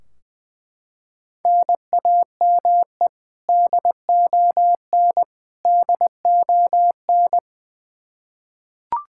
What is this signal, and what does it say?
Morse code, heard as a keyed tone, about 20 wpm, 700 Hz. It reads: NAME DON DON
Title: Practice QSO - Exchange 1 - Beginning - 20wpm